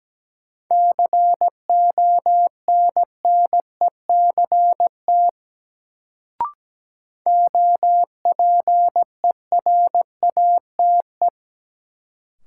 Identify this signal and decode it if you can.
Morse code, heard as a keyed tone, about 17 wpm, 700 Hz. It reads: CONNECT OPERATE
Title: Sentences from Top 1500 Words - New Word Drill Alt - 17wpm